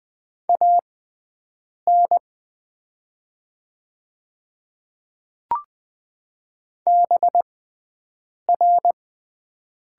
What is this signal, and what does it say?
Morse code, heard as a keyed tone, about 20 wpm, spaced as for 7 wpm, 700 Hz. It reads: AN BR
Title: 6x Character Spacing - Top 500 Words - 20wpm